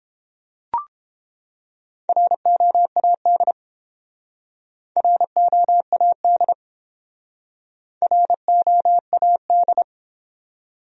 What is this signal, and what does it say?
Morse code, heard as a keyed tone, about 30 wpm, 700 Hz. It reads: ROAD ROAD ROAD